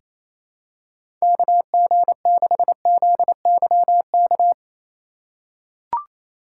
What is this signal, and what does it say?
Morse code, heard as a keyed tone, about 28 wpm, 700 Hz. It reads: KG6ZYK
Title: USA Call Signs - 3x Character Spacing - 28wpm